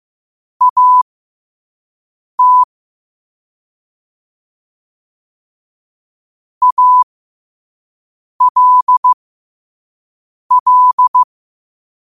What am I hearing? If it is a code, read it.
Morse code, heard as a keyed tone, about 15 wpm, spaced as for 5 wpm, 1000 Hz. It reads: AT ALL
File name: qotdfiles / 5wpm